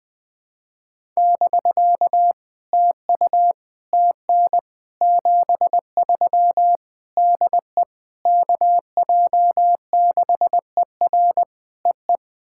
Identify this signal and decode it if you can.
Morse code, heard as a keyed tone, about 20 wpm, 700 Hz. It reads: <BK> TU TN 73 DE KJ6ER EE
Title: POTA QSO Practice - 20wpm